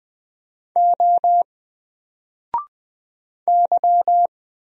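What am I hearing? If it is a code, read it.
Morse code, heard as a keyed tone, about 20 wpm, 700 Hz. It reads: O Y
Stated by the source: Single letter and number Morse drill at 20 wpm, from N4LCD.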